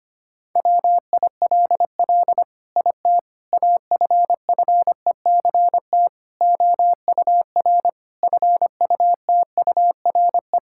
Morse code, heard as a keyed tone, about 25 wpm, 700 Hz. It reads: WILL IT AFFECT OUR FUTURE
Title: Sentences from Top 1000 Words - Review Part 6 - 25wpm